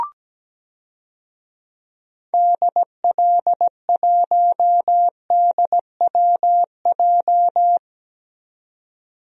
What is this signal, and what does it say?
Morse code, heard as a keyed tone, about 17 wpm, 700 Hz. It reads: DL1DWJ
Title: DXCC-100 Call Signs - 6 Characters - 17wpm